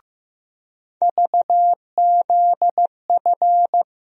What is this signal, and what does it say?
Morse code, heard as a keyed tone, about 15 wpm, 700 Hz. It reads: VZF
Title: USA Call Sign Postfixes - 3 Letters 15wpm